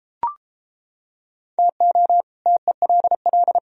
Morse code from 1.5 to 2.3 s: TO

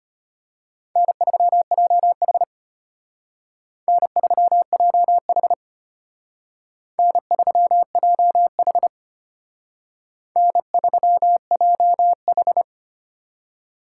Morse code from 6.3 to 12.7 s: N3JH N3JH

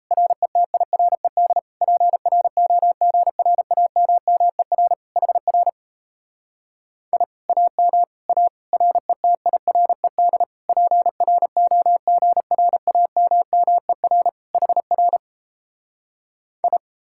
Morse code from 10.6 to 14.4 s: PROGRAMMER

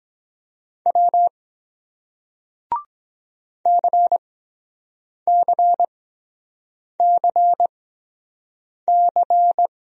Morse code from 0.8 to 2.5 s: W

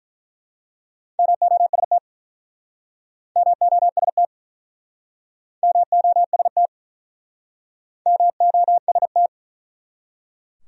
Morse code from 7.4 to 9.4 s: MOST

